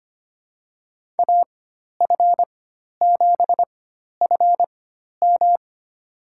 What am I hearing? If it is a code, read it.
Morse code, heard as a keyed tone, about 25 wpm, spaced as for 12 wpm, 700 Hz. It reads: AF7FM
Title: USA Call Signs - 4x Character Spacing - 25wpm